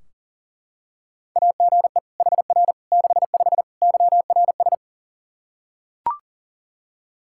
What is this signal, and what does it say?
Morse code, heard as a keyed tone, about 40 wpm, 700 Hz. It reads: AGE HR 65 YRS
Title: Practice QSO - Exchange 3 - More Info - 40wpm